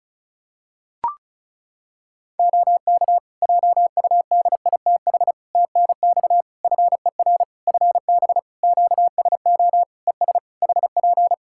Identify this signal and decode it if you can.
Morse code, heard as a keyed tone, about 36 wpm, 700 Hz. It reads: OK JUDITH TNX FER FB QSO ES HP
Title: Practice QSO - Exchange 4 - Ending - 35wpm